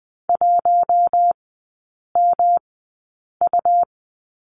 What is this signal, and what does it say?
Morse code, heard as a keyed tone, about 20 wpm, spaced as for 8 wpm, 690 Hz. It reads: 1MU
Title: 133 Lesson 22e - Question Mark - Three Characters 20wpm